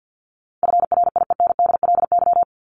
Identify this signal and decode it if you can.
Morse code, heard as a keyed tone, about 49 wpm, 700 Hz. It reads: FRIENDLY
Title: LY Suffix - Top 100 Words with Base Words 50wpm